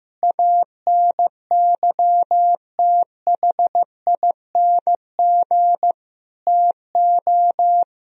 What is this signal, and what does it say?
Morse code, heard as a keyed tone, about 15 wpm, 700 Hz. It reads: ANYTHING TO
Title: Sentences from Top 1000 Words - Review Part 1 - 15wpm